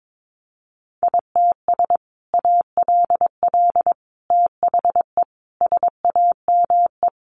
Morse code, heard as a keyed tone, about 22 wpm, 700 Hz. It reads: ITS ALL THE SAME
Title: Sentences from Top 600 Words 22wpm